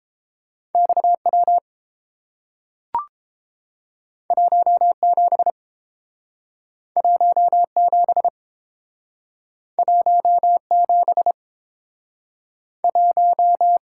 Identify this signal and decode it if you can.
Morse code, heard as a keyed tone, about 30 wpm, 700 Hz. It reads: XW 17 17 17 1